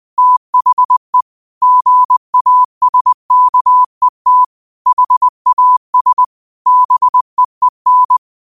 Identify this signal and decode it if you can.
Morse code, heard as a keyed tone, about 20 wpm, 1000 Hz. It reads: THE GASKET HAS BEEN
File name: qotdfiles / 20wpm